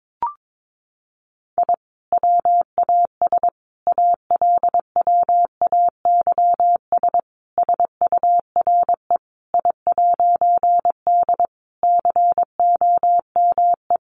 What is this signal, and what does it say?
Morse code, heard as a keyed tone, about 22 wpm, 700 Hz. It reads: I WAS ALWAYS SURE I'D COME
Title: A Sound Of Thunder 22wpm